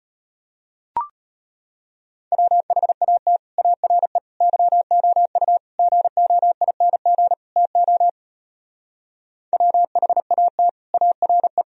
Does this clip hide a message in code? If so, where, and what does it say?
Yes: Morse code from 2.3 to 11.6 s, heard as a keyed tone, about 33 wpm, 700 Hz. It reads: WHAT ARE YOU GOING TO WHAT ARE